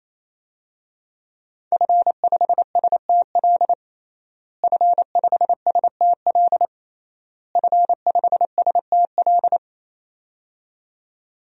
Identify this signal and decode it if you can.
Morse code, heard as a keyed tone, about 27 wpm, 700 Hz. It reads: F5STL F5STL F5STL
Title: DXCC-100 Call Signs - 5 Characters - Repeated 3x - 28wpm